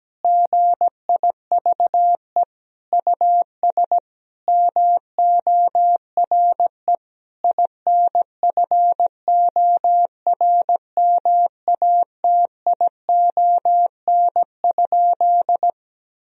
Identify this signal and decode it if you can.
Morse code, heard as a keyed tone, about 17 wpm, 700 Hz. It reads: GIVE US MORE INFORMATION?
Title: Sentences from Top 1000 Words - Review Part 3 - 17wpm